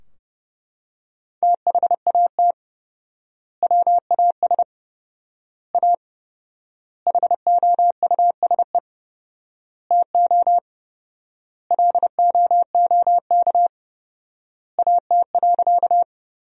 Morse code, heard as a keyed tone, about 30 wpm, 700 Hz. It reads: THAT WAS A HOUSE TO LOOK AT.
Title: 4x Word-Spacing - Sentences from Top 200 Words - 30wpm